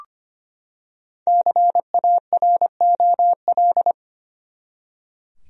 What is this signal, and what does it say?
Morse code, heard as a keyed tone, about 25 wpm, 700 Hz. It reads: CAROL
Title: USA Common Names - Max 5 Letters - 25wpm